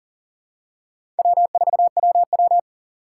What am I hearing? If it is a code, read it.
Morse code, heard as a keyed tone, about 37 wpm, 700 Hz. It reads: W4WW
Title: USA Call Signs - 3x Character Spacing - 40wpm